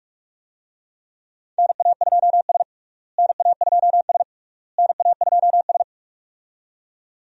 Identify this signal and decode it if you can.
Morse code, heard as a keyed tone, about 41 wpm, 700 Hz. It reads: NA2S NA2S NA2S
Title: USA Call Signs - 4 Characters - Repeated 3x - 45wpm